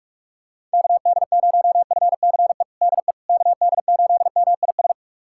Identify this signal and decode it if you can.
Morse code, heard as a keyed tone, about 46 wpm, 690 Hz. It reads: KD0FCE DE KD8MIS
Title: USA Callsign DE Callsign 45wpm